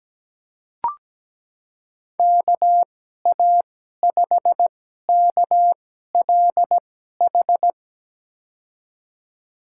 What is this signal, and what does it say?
Morse code, heard as a keyed tone, about 17 wpm, spaced as for 12 wpm, 700 Hz. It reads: KA5KLH
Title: USA Call Signs - 6 Characters - 2x Character Spacing - 17wpm